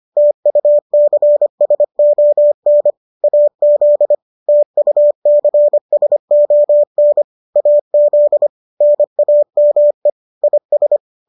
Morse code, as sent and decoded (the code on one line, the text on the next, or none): - ..- -.-. ... --- -. / .- --.. / - ..- -.-. ... --- -. / .- --.. / -. .- -- . / .. ...
TUCSON AZ TUCSON AZ NAME IS